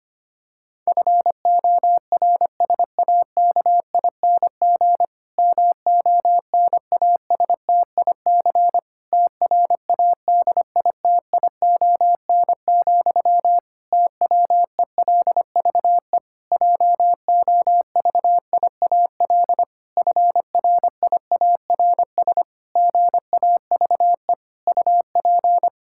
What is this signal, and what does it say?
Morse code, heard as a keyed tone, about 25 wpm, 700 Hz. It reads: FORSAKING MONASTIC TRADITION, TWELVE JOVIAL FRIARS GAVE UP
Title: Pangram Challenge v2 - 25wpm